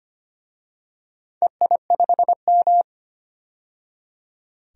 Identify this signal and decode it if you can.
Morse code, heard as a keyed tone, about 25 wpm, 700 Hz. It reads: EI5M
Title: DXCC-100 Call Signs - 4 Characters - 25wpm